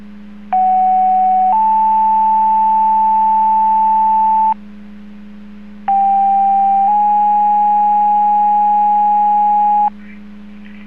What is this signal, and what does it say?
Two-tone page, heard as a one-second tone, then a three-second tone, about 750 then 880 Hz, about 790 then 830 Hz. It is heard from 0.5 to 9.9 s.